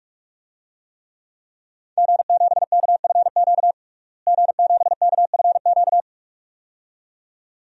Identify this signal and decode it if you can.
Morse code, heard as a keyed tone, about 46 wpm, 700 Hz. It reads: G7KFQ G7KFQ
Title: DXCC-100 Call Signs - 5 Characters - Repeated 2x - 45wpm